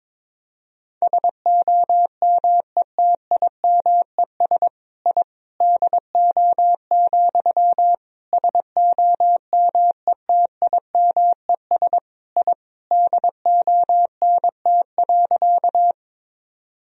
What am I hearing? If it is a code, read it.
Morse code, heard as a keyed tone, about 22 wpm, 700 Hz. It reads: SOMETIMES I DO, SOMETIMES I DONT.
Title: Sentences from Top 1000 Words - Review Part 3 - 22wpm